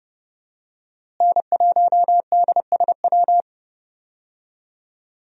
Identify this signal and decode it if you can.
Morse code, heard as a keyed tone, about 31 wpm, 700 Hz. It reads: N1DSW
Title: USA Call Signs - 5 Characters - Repeated 3x - 30wpm